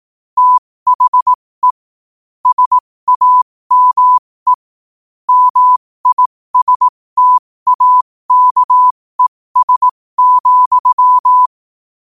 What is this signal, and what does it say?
Morse code, heard as a keyed tone, about 18 wpm, spaced as for 15 wpm, 1000 Hz. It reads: THE SAME MISTAKES,
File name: qotdfiles / 15wpm